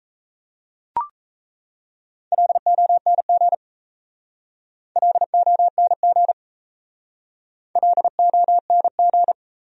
Morse code, heard as a keyed tone, about 37 wpm, 700 Hz. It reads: LONG LONG LONG